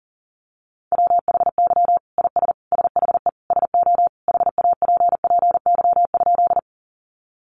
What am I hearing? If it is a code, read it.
Morse code, heard as a keyed tone, about 40 wpm, 700 Hz. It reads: WHY IS SHE SO HAPPY?
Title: Sentences from Top 1000 Words 40wpm